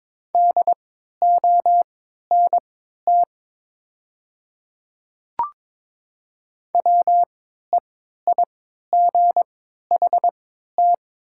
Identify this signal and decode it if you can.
Morse code, heard as a keyed tone, about 22 wpm, spaced as for 12 wpm, 700 Hz. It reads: DONT WEIGHT
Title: 3x Character Spacing - Top 500 Words - 22wpm